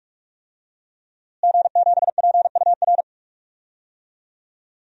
Morse code, heard as a keyed tone, about 46 wpm, 700 Hz. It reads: G7PUR
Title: DXCC-100 Call Signs - 5 Characters - Repeated 3x - 45wpm